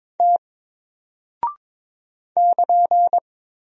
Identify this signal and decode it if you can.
Morse code, heard as a keyed tone, about 22 wpm, 700 Hz. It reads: T <KN>
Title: Single Character (Letter:Number:Punctuation:Prosign) Alt 22wpm